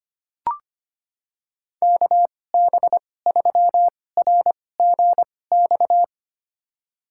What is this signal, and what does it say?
Morse code, heard as a keyed tone, about 25 wpm, spaced as for 18 wpm, 700 Hz. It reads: KB3RGX
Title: USA Call Signs - 6 Characters - 2x Character Spacing - 25wpm